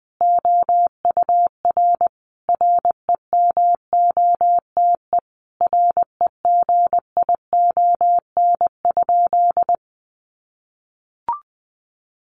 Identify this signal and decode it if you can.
Morse code, heard as a keyed tone, about 20 wpm, 700 Hz. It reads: OUR REMOTE REGION?
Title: Sentences from Top 1500 Words 20wpm